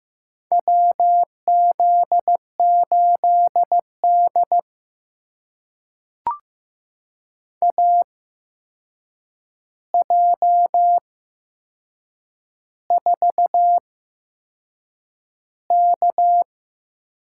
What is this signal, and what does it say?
Morse code, heard as a keyed tone, about 15 wpm, spaced as for 6 wpm, 700 Hz. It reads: WZ8D AJ4K
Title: USA Call Signs - 8x Character Spacing - 15wpm